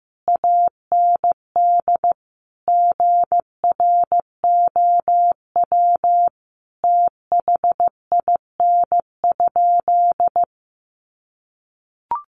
Morse code, heard as a keyed tone, about 15 wpm, 700 Hz. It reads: AND GROW THIN?
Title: Sentences from Top 1200 Words 15wpm